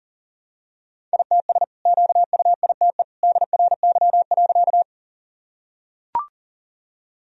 Morse code, heard as a keyed tone, about 40 wpm, 700 Hz. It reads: ITS QUITE DRY.
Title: Sentences from Top 1000 Words - Review Part 3 - 40wpm